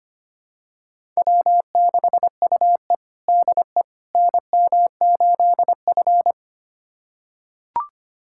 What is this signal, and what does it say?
Morse code, heard as a keyed tone, about 25 wpm, 700 Hz. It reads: W6UE DE NM8F
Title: USA Callsign DE Callsign - 4 Characters - 25wpm